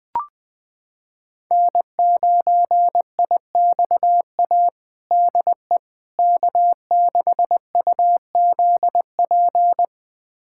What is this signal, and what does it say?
Morse code, heard as a keyed tone, about 20 wpm, 700 Hz. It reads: N9IXA DE K6UZP
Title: USA Callsign DE Callsign - 5 Characters - 20wpm